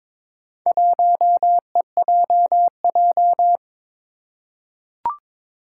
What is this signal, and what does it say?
Morse code, heard as a keyed tone, about 22 wpm, 700 Hz. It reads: 1EJJ
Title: Four Challenging Letter-Number Combinations 22wpm